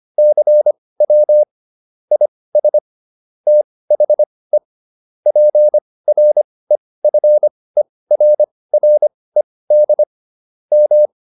Morse code, heard as a keyed tone, about 25 wpm, spaced as for 18 wpm, 600 Hz. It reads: CW IS THE PREFERRED M